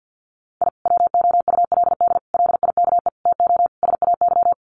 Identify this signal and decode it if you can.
Morse code, heard as a keyed tone, about 50 wpm, 700 Hz. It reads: I WOULD LIKE TO SAY
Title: Sets of 5 Words Encore2 50wpm